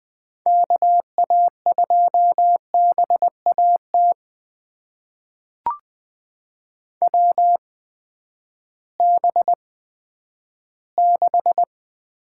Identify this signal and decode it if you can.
Morse code, heard as a keyed tone, about 20 wpm, spaced as for 11 wpm, 700 Hz. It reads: KA2BAT WB6
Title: USA Call Signs - 6 Characters - 8x Character Spacing - 20wpm